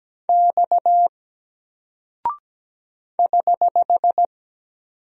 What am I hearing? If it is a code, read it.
Morse code, heard as a keyed tone, about 17 wpm, 700 Hz. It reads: X <HH>
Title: Single Character (Letter:Number:Punctuation:Prosign) Alt 17wpm